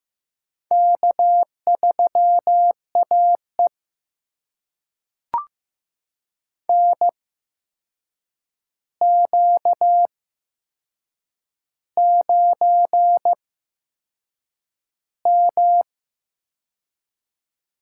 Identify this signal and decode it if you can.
Morse code, heard as a keyed tone, about 15 wpm, spaced as for 6 wpm, 700 Hz. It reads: K3AE NQ9M
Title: USA Call Signs - 8x Character Spacing - 15wpm